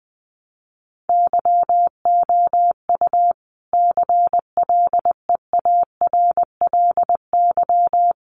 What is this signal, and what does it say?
Morse code, heard as a keyed tone, about 20 wpm, 700 Hz. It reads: YOU CLEARLY
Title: Sentences from Top 1300 Words 20wpm